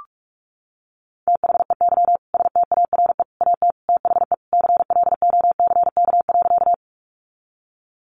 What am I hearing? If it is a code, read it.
Morse code, heard as a keyed tone, about 44 wpm, 700 Hz. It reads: THEY STARE AT THE CLOCK.
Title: Easy Phrases 45wpm